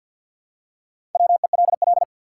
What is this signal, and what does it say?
Morse code, heard as a keyed tone, about 49 wpm, 700 Hz. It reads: WELL
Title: Top 180 QSO Elements 50wpm